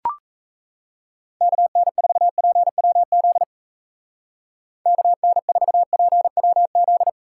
Morse code, heard as a keyed tone, about 34 wpm, 700 Hz. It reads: KN4PWZ KN4PWZ
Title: Speed Racing - USA Call Signs 28wpm